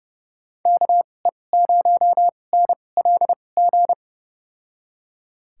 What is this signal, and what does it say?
Morse code, heard as a keyed tone, about 32 wpm, spaced as for 22 wpm, 700 Hz. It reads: KE0NLG
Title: USA Call Signs - 6 Characters - 2x Character Spacing - 30wpm